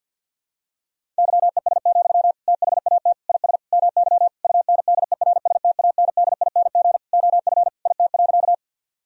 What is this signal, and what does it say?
Morse code, heard as a keyed tone, about 47 wpm, 700 Hz. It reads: YES, THAT IS MY UNDERSTANDING OF IT.
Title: Sentences from Top 1000 Words - Review Part 3 - 50wpm